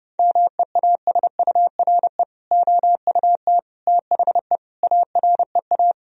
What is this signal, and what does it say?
Morse code, heard as a keyed tone, about 32 wpm, 700 Hz. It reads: MEASURE OUT THE AREA
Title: Sentences from Top 1000 Words - Review Part 2 - 30wpm